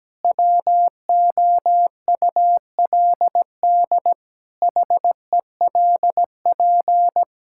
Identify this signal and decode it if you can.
Morse code, heard as a keyed tone, about 17 wpm, 700 Hz. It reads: WOULD HELP
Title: Sentences from Top 1000 Words - Review Part 5 - 17wpm